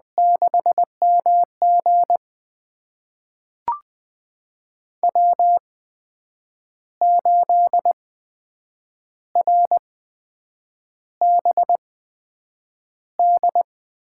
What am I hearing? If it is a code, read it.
Morse code, heard as a keyed tone, about 20 wpm, spaced as for 7 wpm, 700 Hz. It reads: 6MG W8RBD
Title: USA Call Signs - 5 Characters - 8x Character Spacing - 20wpm